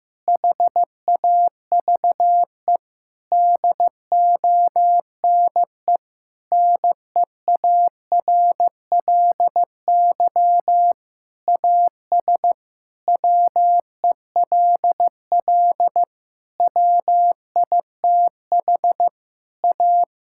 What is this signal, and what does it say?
Morse code, heard as a keyed tone, about 15 wpm, 700 Hz. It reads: HAVE DONE NEARLY AS WELL WITH A
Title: Sentences from Top 1600 Words 15wpm